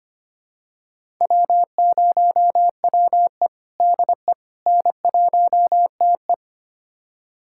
Morse code, heard as a keyed tone, about 25 wpm, 700 Hz. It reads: W0WE DE N1TE
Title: USA Callsign DE Callsign - 4 Characters - 25wpm